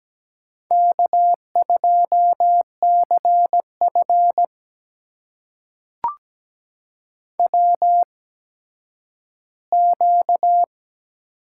Morse code, heard as a keyed tone, about 17 wpm, spaced as for 10 wpm, 700 Hz. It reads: K2CF WQ
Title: USA Call Signs - 4 Characters - 8x Character Spacing - 17wpm